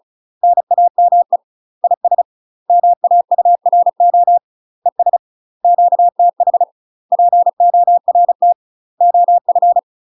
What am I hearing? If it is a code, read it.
Morse code, heard as a keyed tone, about 34 wpm, 700 Hz. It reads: NAME IS MAURO ES QTH PORT OF
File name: UK MASTER QSOs TRAINING / 35WPM / UK MASTER QSOs 0021